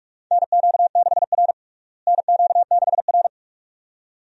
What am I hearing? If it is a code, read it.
Morse code, heard as a keyed tone, about 46 wpm, 700 Hz. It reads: NQ6R NQ6R
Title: USA Call Signs - 4 Characters - Repeated 2x - 45wpm